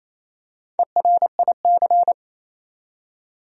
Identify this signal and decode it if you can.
Morse code, heard as a keyed tone, about 28 wpm, 700 Hz. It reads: ERIC
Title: USA Common Names - Max 4 Letters - 28wpm